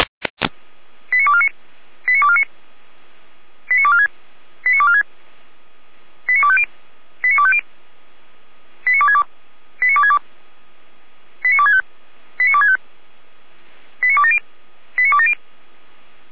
Five-tone selective call (ZVEI), heard as quick runs of five tones, tones of about 70 ms. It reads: 89249 89249 89246 89246 89250 89250 89262 89262 89265 89265 89270 89270